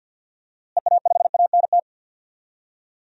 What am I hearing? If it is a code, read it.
Morse code, heard as a keyed tone, about 49 wpm, 700 Hz. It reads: EA5ANT
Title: DXCC-100 Call Signs - 6 Characters - Repeated 2x - 50wpm